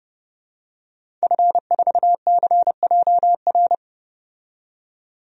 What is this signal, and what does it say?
Morse code, heard as a keyed tone, about 29 wpm, 700 Hz. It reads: F4CJR
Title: DXCC-100 Call Signs - 5 Characters - 30wpm